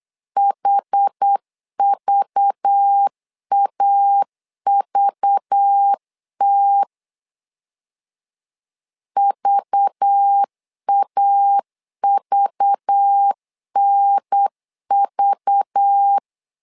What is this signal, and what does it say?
Morse code, heard as a keyed tone, about 8 wpm, 800 Hz. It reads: HVAVT VAVNV
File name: LEZIONI / 5 LBVC / 5 LBV 8wpm 2837